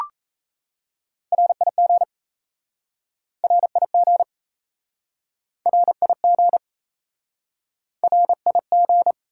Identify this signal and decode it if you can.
Morse code, heard as a keyed tone, about 38 wpm, 700 Hz. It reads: RIG RIG RIG RIG